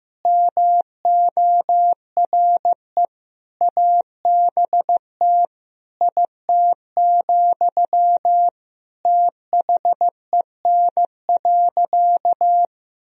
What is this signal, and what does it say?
Morse code, heard as a keyed tone, about 15 wpm, 700 Hz. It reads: MORE ABT IT, THEN.
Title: Sentences from Top 200 Words and Abbreviations 15wpm